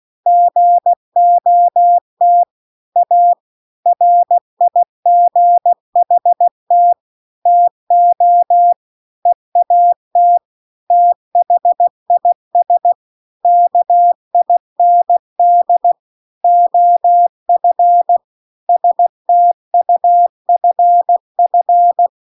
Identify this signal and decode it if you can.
Morse code, heard as a keyed tone, about 16 wpm, 700 Hz. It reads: GOT A RIGHT TO EAT THIS KIND OF STUFF